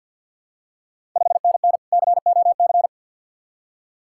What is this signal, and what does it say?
Morse code, heard as a keyed tone, about 53 wpm, 700 Hz. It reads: 5NN COC
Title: Ontario QSO Party Exchange 50wpm